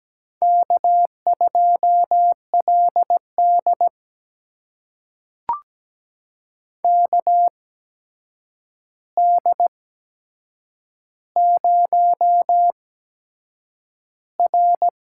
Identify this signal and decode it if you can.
Morse code, heard as a keyed tone, about 17 wpm, spaced as for 7 wpm, 700 Hz. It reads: K2LD KD0R
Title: USA Call Signs - 4 Characters - 8x Character Spacing - 17wpm